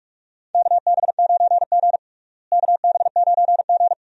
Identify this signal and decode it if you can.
Morse code, heard as a keyed tone, about 44 wpm, 700 Hz. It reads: KB9G KB9G
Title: USA Call Signs - 4 Characters - Repeated 2x - 45wpm